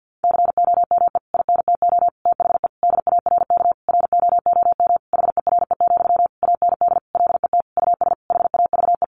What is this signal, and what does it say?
Morse code, heard as a keyed tone, about 50 wpm, 700 Hz. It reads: COME INTO THE DARK ROOM HERE, AND LET US HAVE
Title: Sentences from Top 1000 Words 50wpm